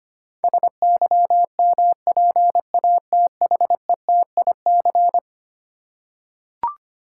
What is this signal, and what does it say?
Morse code, heard as a keyed tone, about 25 wpm, 700 Hz. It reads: SYMPATHETIC
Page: morse-code-ninja-practice-25wpm